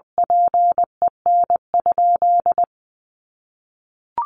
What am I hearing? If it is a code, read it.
Morse code, heard as a keyed tone, about 19 wpm, 700 Hz. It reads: PEN?
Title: Sentences from Top 1100 Words 20wpm